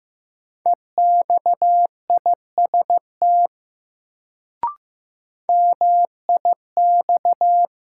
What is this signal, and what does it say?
Morse code, heard as a keyed tone, about 16 wpm, 700 Hz. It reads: EXIST MIX
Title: Sentences from Top 1100 Words - New Word Drill Alt - 15wpm